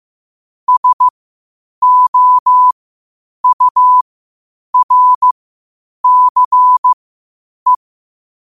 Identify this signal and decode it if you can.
Morse code, heard as a keyed tone, about 15 wpm, spaced as for 8 wpm, 1000 Hz. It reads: SOURCE
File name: qotdfiles / 8wpm